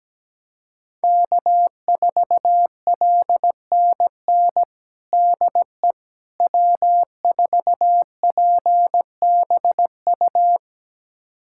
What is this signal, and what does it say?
Morse code, heard as a keyed tone, about 17 wpm, 700 Hz. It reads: K4LNN DE W4PBU
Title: USA Callsign DE Callsign - 5 Characters - 17wpm